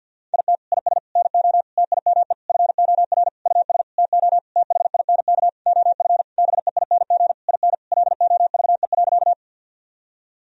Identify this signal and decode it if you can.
Morse code, heard as a keyed tone, about 52 wpm, 700 Hz. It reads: IT IS NO TIME FOR US TO THINK OF BEING IN LOVE.
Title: Sentences from Top 1000 Words - Review Part 3 - 50wpm